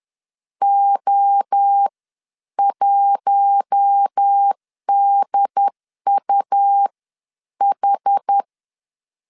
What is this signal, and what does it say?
Morse code, heard as a keyed tone, about 10 wpm, spaced as for 8 wpm, 800 Hz. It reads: O1DUH
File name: CW / DET / 1 6 1500C S C 10wpm 4252